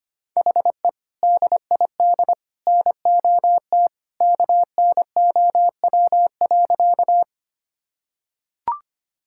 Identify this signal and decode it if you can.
Morse code, heard as a keyed tone, about 25 wpm, 700 Hz. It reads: HE DID NOT KNOW.
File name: practice / 25wpm / Sentences with 2-Word Phrases - 25wpm